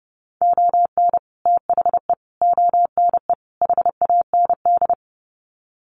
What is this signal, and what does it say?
Morse code, heard as a keyed tone, about 30 wpm, 700 Hz. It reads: ON THE ONE HAND
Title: Sets of 4 Words 30wpm